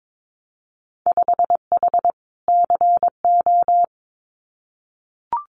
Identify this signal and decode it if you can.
Morse code, heard as a keyed tone, about 22 wpm, 700 Hz. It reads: 5H CO
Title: Winter Field Day Exchanges Encore 22wpm